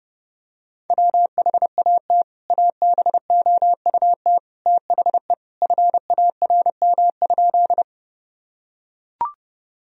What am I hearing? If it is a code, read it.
Morse code, heard as a keyed tone, about 32 wpm, 700 Hz. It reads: WHAT ABOUT THE FARM?
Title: Sentences from Top 1000 Words - Review Part 2 - 30wpm